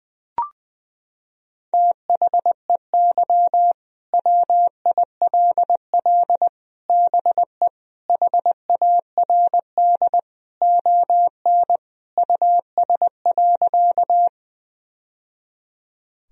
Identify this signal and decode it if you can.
Morse code, heard as a keyed tone, about 20 wpm, 700 Hz. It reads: THEY WILL BE HARD ON US.
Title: Sentences from Top 1000 Words - Review Part 1 - 20wpm